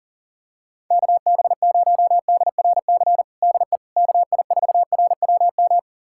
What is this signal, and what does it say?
Morse code, heard as a keyed tone, about 40 wpm, 700 Hz. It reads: KB0DRC DE KI4RWM
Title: USA Callsign DE Callsign - 6 Characters - 40wpm